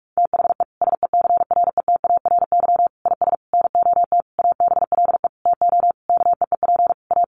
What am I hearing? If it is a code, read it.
Morse code, heard as a keyed tone, about 44 wpm, 700 Hz. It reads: THE SECRETARY IS NOT ABLE TO KEEP A
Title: Sentences from Top 1300 Words 45wpm